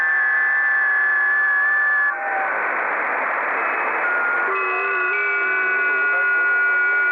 FT8.
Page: 5 min FT8 Test File